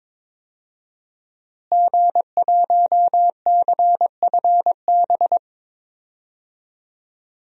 Morse code, heard as a keyed tone, about 22 wpm, 700 Hz. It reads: G1CFB
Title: DXCC-100 Call Signs - 5 Characters - Repeated 2x - 22wpm